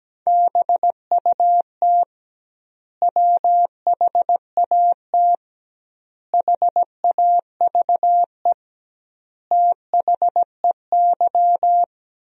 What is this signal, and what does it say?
Morse code, heard as a keyed tone, about 17 wpm, 700 Hz. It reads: BUT WHAT HAVE THEY
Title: 2x Word-Spacing - Sentences from Top 100 Words - 17wpm